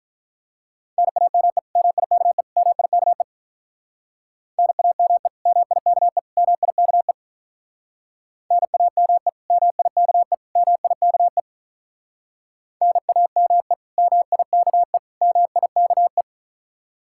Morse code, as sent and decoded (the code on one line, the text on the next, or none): -. .- -- . / -- .. -.- . / -- .. -.- . / -. .- -- . / -- .. -.- . / -- .. -.- . / -. .- -- . / -- .. -.- . / -- .. -.- . / -. .- -- . / -- .. -.- . / -- .. -.- .
NAME MIKE MIKE NAME MIKE MIKE NAME MIKE MIKE NAME MIKE MIKE